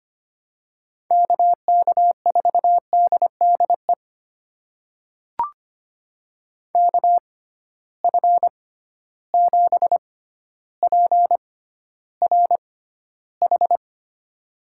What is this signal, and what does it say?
Morse code, heard as a keyed tone, about 25 wpm, spaced as for 13 wpm, 700 Hz. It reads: KK4DDE KF7PRH
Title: USA Call Signs - 6x Character Spacing - 25wpm